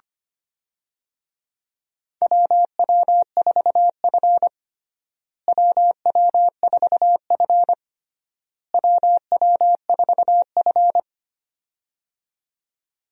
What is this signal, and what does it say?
Morse code, heard as a keyed tone, about 25 wpm, 700 Hz. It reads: WW4F WW4F WW4F
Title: USA Call Signs - 4 Characters - Repeated 3x - 25wpm